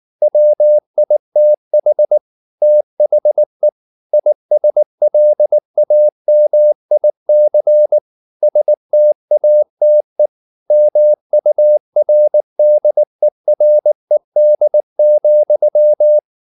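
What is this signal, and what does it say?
Morse code, heard as a keyed tone, about 19 wpm, 600 Hz. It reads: WITH THE ISLAMIC STATE MURDERED,
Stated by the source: Morse code at 19 wpm from CW med Gnister, reading Ken Follett's novel Never in English.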